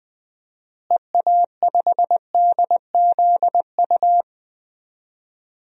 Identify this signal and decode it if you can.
Morse code, heard as a keyed tone, about 20 wpm, 700 Hz. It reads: EA5DZU
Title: DXCC-100 Call Signs - 6 Characters - Repeated 3x - 20wpm